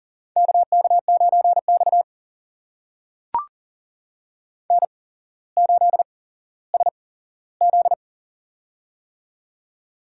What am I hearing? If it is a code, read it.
Morse code, heard as a keyed tone, about 40 wpm, spaced as for 17 wpm, 700 Hz. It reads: KK9X N8SZ
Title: USA Call Signs - 4 Characters - 8x Character Spacing - 40wpm